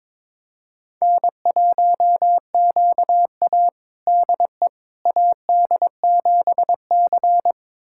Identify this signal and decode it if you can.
Morse code, heard as a keyed tone, about 22 wpm, 700 Hz. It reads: N1QA DE AD7C
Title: USA Callsign DE Callsign - 4 Characters - 22wpm